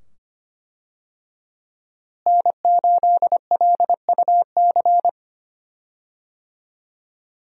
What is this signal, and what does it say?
Morse code, heard as a keyed tone, about 25 wpm, 700 Hz. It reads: N8LUC
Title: USA Call Signs - 5 Characters - Repeated 2x - 25wpm